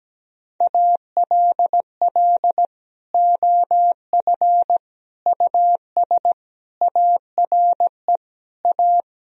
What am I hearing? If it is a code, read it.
Morse code, heard as a keyed tone, about 17 wpm, 700 Hz. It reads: ALL OF US ARE A
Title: Sentences from Top 1000 Words - Review Part 3 - 17wpm